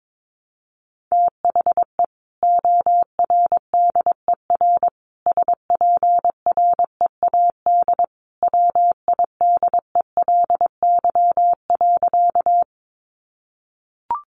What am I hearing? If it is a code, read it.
Morse code, heard as a keyed tone, about 22 wpm, 700 Hz. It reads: THE ORDER SPREAD WIDELY.
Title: Sentences from Top 1200 Words 22wpm